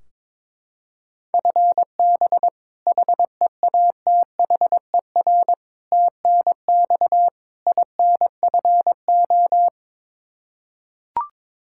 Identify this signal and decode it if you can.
Morse code, heard as a keyed tone, about 22 wpm, 700 Hz. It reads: FB HEATHER TNX INFO
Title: Practice QSO - Exchange 3 - More Info - 22wpm